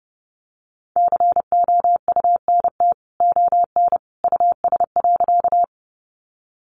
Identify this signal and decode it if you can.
Morse code, heard as a keyed tone, about 30 wpm, 700 Hz. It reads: COUNT ON US.
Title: Sentences from Top 1500 Words 30wpm